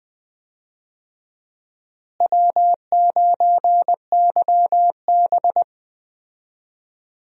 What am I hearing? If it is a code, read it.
Morse code, heard as a keyed tone, about 20 wpm, 700 Hz. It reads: W9YB
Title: USA Call Signs - 4 Characters - Repeated 2x - 20wpm